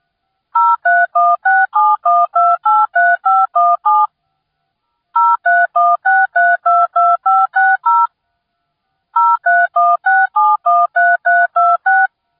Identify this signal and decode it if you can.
DTMF, heard as telephone touch tones, tones of about 220 ms. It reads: 031671283517 0316322560 0316713326